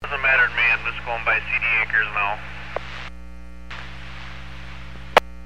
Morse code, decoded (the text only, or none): none